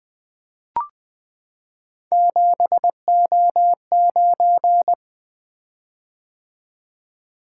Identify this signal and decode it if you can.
Morse code, heard as a keyed tone, about 20 wpm, 700 Hz. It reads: 7O9
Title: Three Letter-Number Combinations - 20wpm